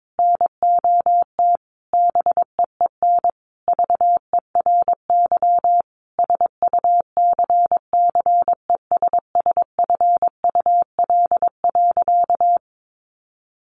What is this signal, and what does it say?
Morse code, heard as a keyed tone, about 22 wpm, 700 Hz. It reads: NOT BEEN VERY SUCCESSFUL.
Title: Sentences from Top 1300 Words 22wpm